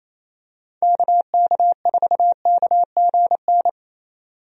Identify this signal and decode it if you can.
Morse code, heard as a keyed tone, about 28 wpm, 700 Hz. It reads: KK4KGN
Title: USA Call Signs - 6 Characters - 28wpm